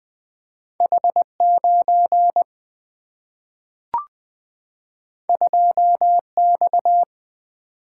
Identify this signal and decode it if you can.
Morse code, heard as a keyed tone, about 20 wpm, 700 Hz. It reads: H9 2X
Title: Single-Letter Single-Number 20wpm